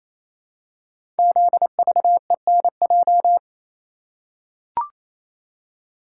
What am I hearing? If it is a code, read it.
Morse code, heard as a keyed tone, about 28 wpm, 700 Hz. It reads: ZVENJ
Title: Code Groups - 5 Letters - 28wpm